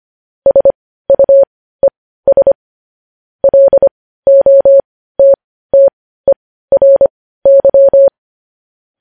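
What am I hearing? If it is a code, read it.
Morse code, heard as a keyed tone, about 25 wpm, spaced as for 15 wpm, 550 Hz. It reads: SUES LOTTERY
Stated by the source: Morse code practice of news headlines at 15 wpm.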